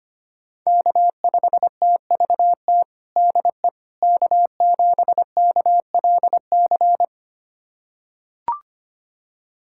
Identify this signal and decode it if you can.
Morse code, heard as a keyed tone, about 25 wpm, 700 Hz. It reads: K5TVT DE K7KLC
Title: USA Callsign DE Callsign - 5 Characters - 25wpm